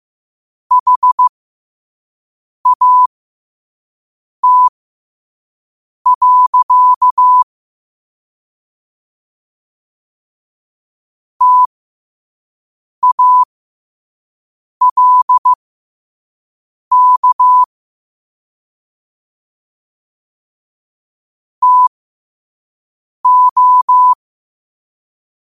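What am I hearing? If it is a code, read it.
Morse code, heard as a keyed tone, about 15 wpm, spaced as for 5 wpm, 1000 Hz. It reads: HAT. TALK TO
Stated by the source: Morse code quote of the day sent at 5 wpm.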